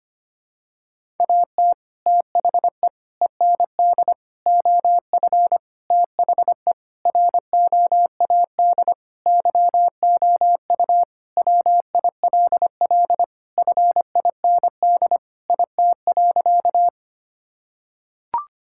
Morse code, heard as a keyed tone, about 25 wpm, 700 Hz. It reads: AT THE END OF THE ROAD YOU WILL FIND IT.
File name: practice / 25wpm / Sentences with 3-Word Phrases - 25wpm